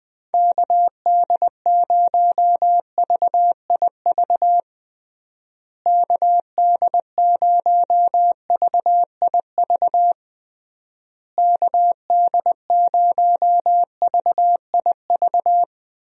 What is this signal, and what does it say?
Morse code, heard as a keyed tone, about 20 wpm, 700 Hz. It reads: KD0VIV KD0VIV KD0VIV
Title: USA Call Signs - 6 Characters - Repeated 3x - 20wpm